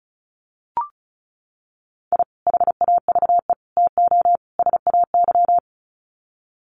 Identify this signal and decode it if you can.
Morse code, heard as a keyed tone, about 34 wpm, 700 Hz. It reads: I HAVE TO SAY
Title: Sets of 4 Words 35wpm